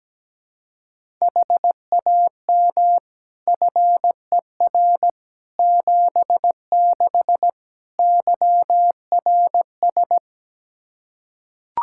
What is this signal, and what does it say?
Morse code, heard as a keyed tone, about 17 wpm, 700 Hz. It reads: HAM FER 76 YRS
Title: Practice QSO - Exchange 3 - More Info - 17wpm